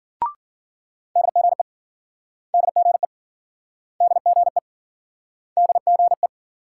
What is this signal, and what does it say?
Morse code, heard as a keyed tone, about 52 wpm, 700 Hz. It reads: DGE DGE DGE DGE